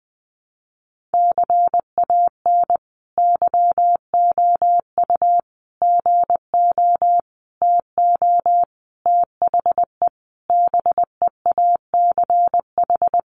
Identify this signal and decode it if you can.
Morse code, heard as a keyed tone, about 19 wpm, 700 Hz. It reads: CAN YOU GO TO THE BEACH
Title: Sentences from Top 1500 Words 20wpm